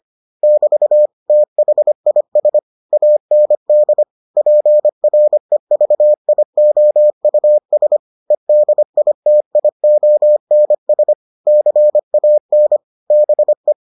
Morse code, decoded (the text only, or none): <BT> THIS AND PREVIOUS EDITIONS CAN BE